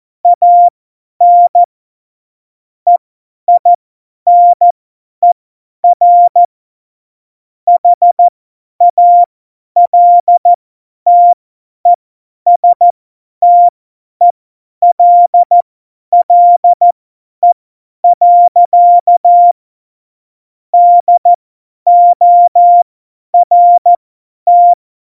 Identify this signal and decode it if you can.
Morse code, heard as a keyed tone, about 14 wpm, spaced as for 10 wpm, 700 Hz. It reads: AN EINER HALTESTELLE. DORT
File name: Morsekurs / Witze / Witz 27. November 2018, 50 F70